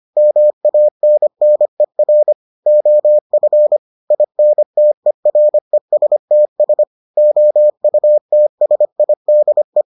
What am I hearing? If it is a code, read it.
Morse code, heard as a keyed tone, about 25 wpm, 600 Hz. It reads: MANNER OF INTERESTS OUTSIDE